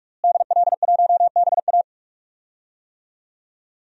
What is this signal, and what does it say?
Morse code, heard as a keyed tone, about 42 wpm, 700 Hz. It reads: DL1BA
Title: DXCC-100 Call Signs - 5 Characters - Repeated 2x - 45wpm